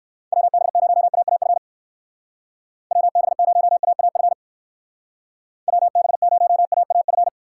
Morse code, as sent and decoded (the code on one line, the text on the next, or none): .-- -... ----- .- .- ..-. / .-- -... ----- .- .- ..-. / .-- -... ----- .- .- ..-.
WB0AAF WB0AAF WB0AAF